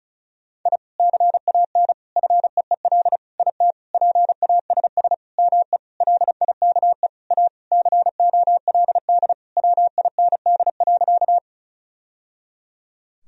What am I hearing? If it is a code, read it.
Morse code, heard as a keyed tone, about 34 wpm, 700 Hz. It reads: I CAN FEEL IT PASS ME LIKE A COLD WIND.
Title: Sentences from Top 1000 Words - Review Part 2 - 35wpm